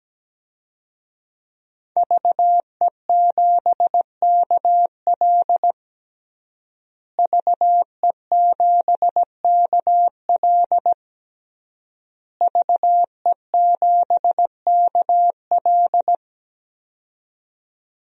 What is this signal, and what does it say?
Morse code, heard as a keyed tone, about 17 wpm, 700 Hz. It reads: VE7KL VE7KL VE7KL
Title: DXCC-100 Call Signs - 5 Characters - Repeated 3x - 17wpm